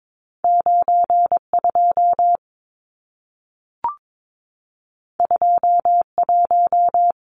Morse code, heard as a keyed tone, about 22 wpm, 700 Hz. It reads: 92 21
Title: Numbers - 2-Digit 22wpm